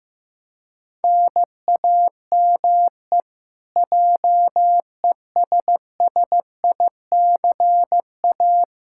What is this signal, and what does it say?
Morse code, heard as a keyed tone, about 16 wpm, 700 Hz. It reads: NAME JESSICA
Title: Practice QSO - Exchange 1 - Beginning - 15wpm